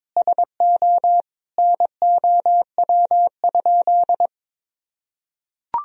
Morse code, heard as a keyed tone, about 22 wpm, 700 Hz. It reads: SO NOW?
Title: Sentences from Top 200 Words and Abbreviations 22wpm